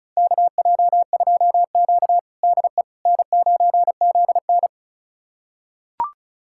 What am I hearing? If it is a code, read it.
Morse code, heard as a keyed tone, about 36 wpm, 700 Hz. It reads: KJ2Q DE N9ZN
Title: USA Callsign DE Callsign - 4 Characters - 35wpm